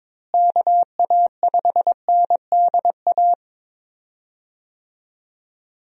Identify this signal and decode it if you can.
Morse code, heard as a keyed tone, about 22 wpm, 700 Hz. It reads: KA5NDA
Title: USA Call Signs - 6 Characters - Repeated 3x - 22wpm